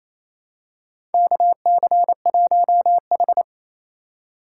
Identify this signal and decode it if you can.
Morse code, heard as a keyed tone, about 28 wpm, 700 Hz. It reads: KC1H
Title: USA Call Signs - 3x Character Spacing - 28wpm